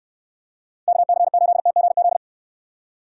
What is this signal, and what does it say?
Morse code, heard as a keyed tone, about 69 wpm, 700 Hz. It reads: KC8TGZ